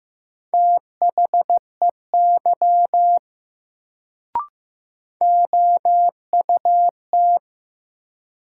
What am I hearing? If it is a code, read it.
Morse code, heard as a keyed tone, about 15 wpm, 700 Hz. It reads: THEY OUT
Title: Sentences from Top 100 Words - New Word Drill - 15wpm